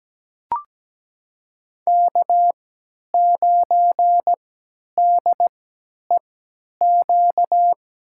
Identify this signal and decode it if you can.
Morse code, heard as a keyed tone, about 17 wpm, spaced as for 10 wpm, 700 Hz. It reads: K9DEQ
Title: USA Call Signs - 5 Characters - 3x Character Spacing - 17wpm